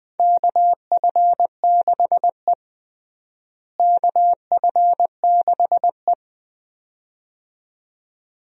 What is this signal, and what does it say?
Morse code, heard as a keyed tone, about 21 wpm, 700 Hz. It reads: KF6E KF6E
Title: USA Call Signs - 4 Characters - Repeated 2x - 20wpm